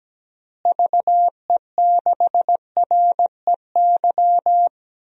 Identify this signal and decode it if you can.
Morse code, heard as a keyed tone, about 17 wpm, 700 Hz. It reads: VE6REY
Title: DXCC-100 Call Signs - 6 Characters - Repeated 3x - 17wpm